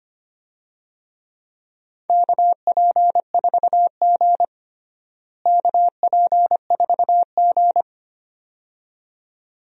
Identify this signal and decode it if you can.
Morse code, heard as a keyed tone, about 25 wpm, 700 Hz. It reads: KP4G KP4G
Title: DXCC-100 Call Signs - 4 Characters - Repeated 2x - 25wpm